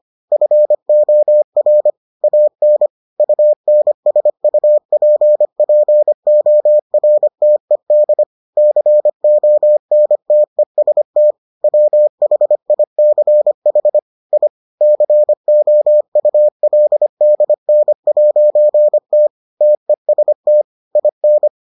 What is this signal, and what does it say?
Morse code, heard as a keyed tone, about 25 wpm, 600 Hz. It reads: FOR AN UNSUPPORTED CONTEST WHICH I COULDN'T TEST IN